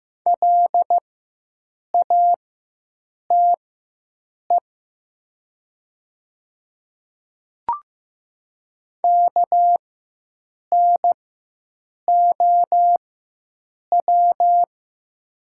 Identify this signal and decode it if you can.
Morse code, heard as a keyed tone, about 15 wpm, spaced as for 7 wpm, 700 Hz. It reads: LATE KNOW
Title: 4x Character Spacing - Top 300 Words - 15wpm